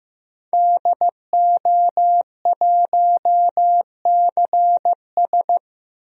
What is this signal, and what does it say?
Morse code, heard as a keyed tone, about 15 wpm, 700 Hz. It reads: DO1CS
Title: DXCC-100 Call Signs - 5 Characters - Repeated 3x - 15wpm